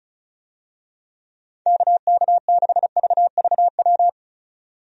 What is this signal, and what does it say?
Morse code, heard as a keyed tone, about 34 wpm, 700 Hz. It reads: KK6VVW
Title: USA Call Signs - 6 Characters - 35wpm